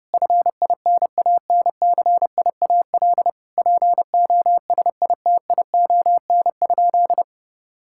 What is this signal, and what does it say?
Morse code, heard as a keyed tone, about 28 wpm, 700 Hz. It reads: FINANCIAL POSITION?
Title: Sentences from Top 1000 Words - Review Part 5 - 30wpm